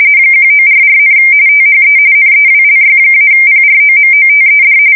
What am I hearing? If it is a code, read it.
RTTY, about 45.45 baud. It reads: 1944, THE NEWLY-PROMOTED
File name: fsk170 / battleshipnj